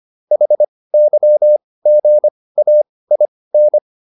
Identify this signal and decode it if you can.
Morse code, heard as a keyed tone, about 25 wpm, spaced as for 18 wpm, 600 Hz. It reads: HYGAIN